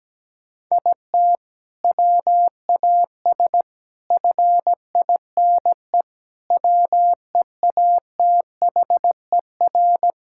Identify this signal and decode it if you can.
Morse code, heard as a keyed tone, about 17 wpm, 700 Hz. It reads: IT WAS FINE WEATHER
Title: Sentences from Top 1000 Words - Review Part 5 - 17wpm